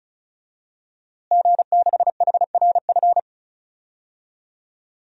Morse code, heard as a keyed tone, about 36 wpm, 700 Hz. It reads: G6HRF
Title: DXCC-100 Call Signs - 5 Characters - Repeated 2x - 35wpm